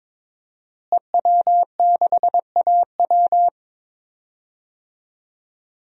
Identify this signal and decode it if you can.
Morse code, heard as a keyed tone, about 22 wpm, 700 Hz. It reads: EW6AW
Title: DXCC-100 Call Signs - 5 Characters - Repeated 2x - 22wpm